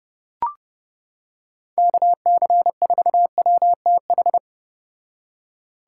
Morse code, heard as a keyed tone, about 30 wpm, 700 Hz. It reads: KC4WTH